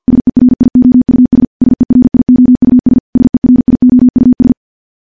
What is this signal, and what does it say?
Morse code, heard as a keyed tone, about 51 wpm, 250 Hz. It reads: SERIOUS SERIOUS SERIOUS
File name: morse / morse-training-50WPM